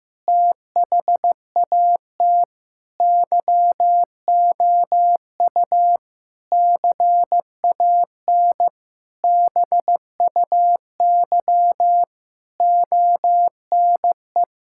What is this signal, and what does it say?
Morse code, heard as a keyed tone, about 16 wpm, 700 Hz. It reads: THAT YOU CAN BUY ONE
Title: Sentences from Top 1000 Words - Review Part 5 - 15wpm